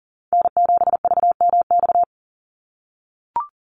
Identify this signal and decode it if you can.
Morse code, heard as a keyed tone, about 40 wpm, 690 Hz. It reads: N7VMX
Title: USA Call Signs 40wpm